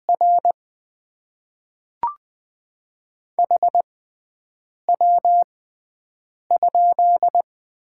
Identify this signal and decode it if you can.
Morse code, heard as a keyed tone, about 20 wpm, spaced as for 7 wpm, 700 Hz. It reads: R HW?
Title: Top 20 QSO Elements - 6x Character Spacing - 20wpm